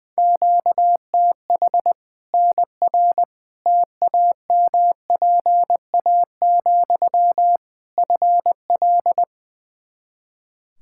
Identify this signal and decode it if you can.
Morse code, heard as a keyed tone, about 21 wpm, 700 Hz. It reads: QTH NR TAMPA, FL
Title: Practice QSO - Exchange 1 - Beginning - 20wpm